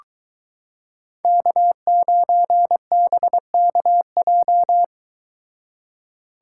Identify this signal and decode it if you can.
Morse code, heard as a keyed tone, about 22 wpm, 700 Hz. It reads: K9BKJ